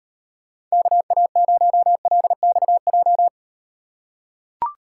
Morse code, heard as a keyed tone, about 38 wpm, 700 Hz. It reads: KA0LXJ